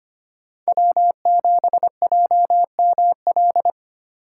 Morse code, heard as a keyed tone, about 29 wpm, 700 Hz. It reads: W7JML
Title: Speed Racing - USA 1x3 Call Signs 25wpm